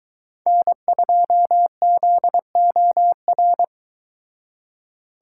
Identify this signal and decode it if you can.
Morse code, heard as a keyed tone, about 22 wpm, 700 Hz. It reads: N2ZOR